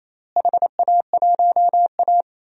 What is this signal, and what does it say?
Morse code, heard as a keyed tone, about 27 wpm, 700 Hz. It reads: HA1A
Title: DXCC-100 Call Signs - 4 Characters - Repeated 2x - 28wpm